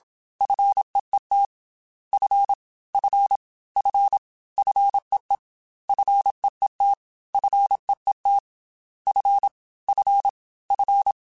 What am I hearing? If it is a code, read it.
Morse code, heard as a keyed tone, about 27 wpm, 800 Hz. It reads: FEET F F F FEE FEET FEET F F F
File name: Single F mp3s / Single F Prefixes 3x2 Twice @25